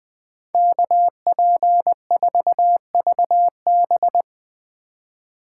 Morse code, heard as a keyed tone, about 20 wpm, 700 Hz. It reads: KP4VB
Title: Speed Racing - USA Call Signs 17wpm